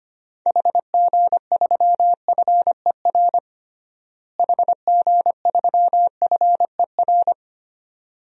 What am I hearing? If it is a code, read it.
Morse code, heard as a keyed tone, about 25 wpm, 700 Hz. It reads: HG3FER HG3FER
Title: DXCC-100 Call Signs - 6 Characters - Repeated 2x - 25wpm